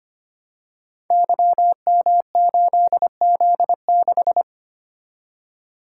Morse code, heard as a keyed tone, about 25 wpm, 700 Hz. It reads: YM8Z6
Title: Code Groups - 5 Letter:Numbers - 25wpm